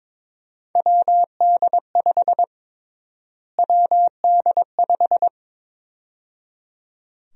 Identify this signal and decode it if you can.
Morse code, heard as a keyed tone, about 22 wpm, 700 Hz. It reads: WD5 WD5
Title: USA Call Sign Prefixes - Repeated 2x - 22wpm